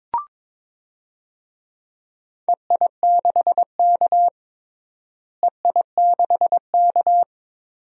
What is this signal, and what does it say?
Morse code, heard as a keyed tone, about 22 wpm, 700 Hz. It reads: EI6K EI6K
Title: DXCC-100 Call Signs - 4 Characters - Repeated 2x - 22wpm